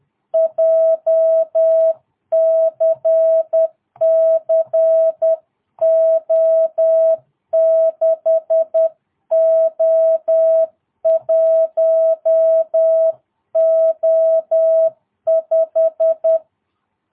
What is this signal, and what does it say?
Morse code, heard as a keyed tone, about 10 wpm, 650 Hz. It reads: JCC O6O1O5
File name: jh8cbh / 2022cw